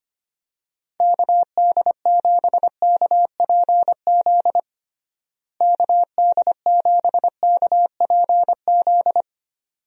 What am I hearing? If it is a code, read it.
Morse code, heard as a keyed tone, about 25 wpm, 700 Hz. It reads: KD7KPZ KD7KPZ
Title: USA Call Signs - 6 Characters - Repeated 2x - 25wpm